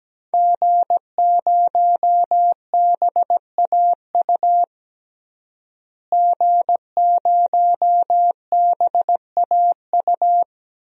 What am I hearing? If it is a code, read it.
Morse code, heard as a keyed tone, about 17 wpm, 700 Hz. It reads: G0BAU G0BAU
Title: DXCC-100 Call Signs - 5 Characters - Repeated 3x - 17wpm